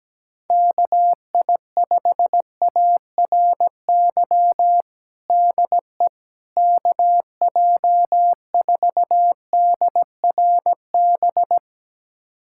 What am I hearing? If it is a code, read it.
Morse code, heard as a keyed tone, about 17 wpm, 700 Hz. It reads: KI5ARY DE KJ4DRB
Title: USA Callsign DE Callsign 17wpm